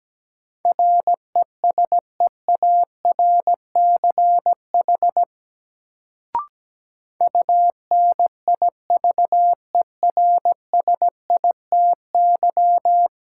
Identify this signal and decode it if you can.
Morse code, heard as a keyed tone, about 17 wpm, 700 Hz. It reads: RESEARCH UNIVERSITY
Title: Sentences from Top 1000 Words - Review Part 4 - New Word Drill - 17wpm